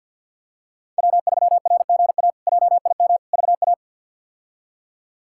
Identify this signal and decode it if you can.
Morse code, heard as a keyed tone, about 48 wpm, 700 Hz. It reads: W3RGA JIM VA
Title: K1USN SST QSO Practice - 50wpm